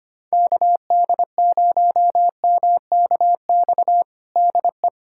Morse code, heard as a keyed tone, about 25 wpm, 700 Hz. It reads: KD0MKX DE
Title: USA Callsign DE Callsign - 6 Characters - 25wpm